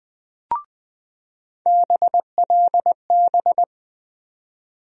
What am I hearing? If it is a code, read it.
Morse code, heard as a keyed tone, about 20 wpm, 700 Hz. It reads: BLB
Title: USA Call Sign Postfixes - 3 Letters 20wpm